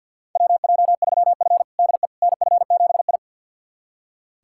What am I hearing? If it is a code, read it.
Morse code, heard as a keyed tone, about 46 wpm, 700 Hz. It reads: WJ3F DE NF7I
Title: USA Callsign DE Callsign - 4 Characters - 50wpm